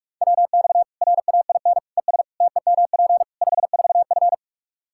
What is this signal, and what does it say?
Morse code, heard as a keyed tone, about 46 wpm, 700 Hz. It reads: WX RAIN ES TEMP 54F
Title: Practice QSO - Exchange 2 - Info - 45wpm